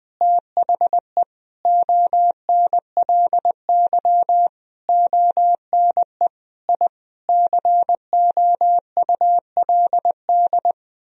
Morse code, heard as a keyed tone, about 20 wpm, 700 Hz. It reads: THE ONLY ONE I COULD
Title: Sentences from Top 1000 Words - Review Part 4 - 20wpm